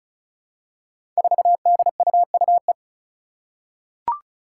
Morse code, heard as a keyed tone, about 36 wpm, 700 Hz. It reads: 4DUUE
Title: Five Challenging Letter-Number Combinations 35wpm